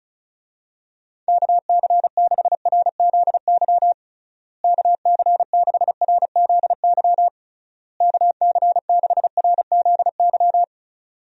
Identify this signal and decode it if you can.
Morse code, heard as a keyed tone, about 36 wpm, 700 Hz. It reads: KC6RZY KC6RZY KC6RZY
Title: USA Call Signs - 6 Characters - Repeated 3x - 35wpm